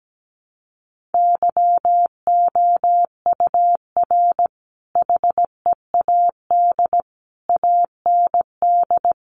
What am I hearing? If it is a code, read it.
Morse code, heard as a keyed tone, about 17 wpm, 700 Hz. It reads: YOUR HEAD AND
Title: Sentences from Top 1300 Words 17wpm